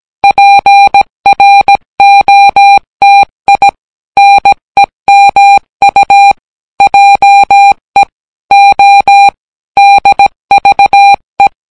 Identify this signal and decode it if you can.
Morse code, heard as a keyed tone, about 18 wpm, 800 Hz. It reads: PROTI NEMU JE O DVE